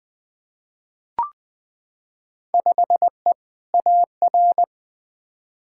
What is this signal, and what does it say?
Morse code, heard as a keyed tone, about 21 wpm, 700 Hz. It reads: 5E AR
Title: Field Day Exchanges Encore - 20wpm